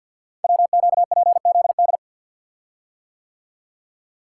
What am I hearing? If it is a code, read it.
Morse code, heard as a keyed tone, about 46 wpm, 700 Hz. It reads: WQPZD
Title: Code Groups - 5 Letter:Numbers - 50wpm